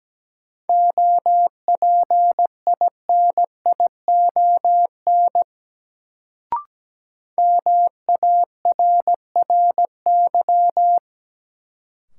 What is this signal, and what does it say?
Morse code, heard as a keyed tone, about 17 wpm, 700 Hz. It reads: OPINION MARRY
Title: Sentences from Top 1200 Words - New Word Drill - 17wpm